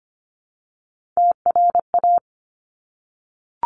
Morse code, heard as a keyed tone, about 25 wpm, 700 Hz. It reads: TRA
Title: Top Three Letter Combinations in Words 25wpm